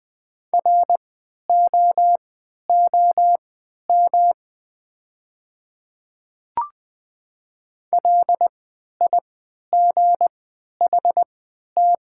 Morse code, heard as a keyed tone, about 20 wpm, spaced as for 11 wpm, 700 Hz. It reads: ROOM LIGHT
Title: 3x Character Spacing - Top 300 Words - 20wpm